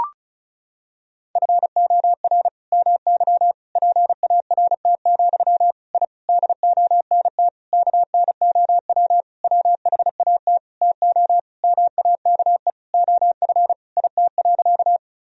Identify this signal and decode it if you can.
Morse code, heard as a keyed tone, about 36 wpm, 700 Hz. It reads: FOR MY PART, I DONT KNOW WHAT TO MAKE OF IT.
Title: Sentences from Top 1000 Words - Review Part 1 - 35wpm